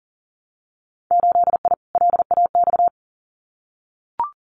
Morse code, heard as a keyed tone, about 40 wpm, 700 Hz. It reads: 8I LAX
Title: Winter Field Day Exchanges Encore 40wpm